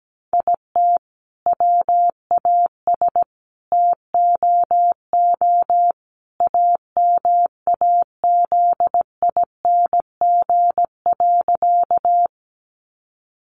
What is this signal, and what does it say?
Morse code, heard as a keyed tone, about 17 wpm, 700 Hz. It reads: IT WAS TOO AMAZING.
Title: Sentences from Top 1500 Words 17wpm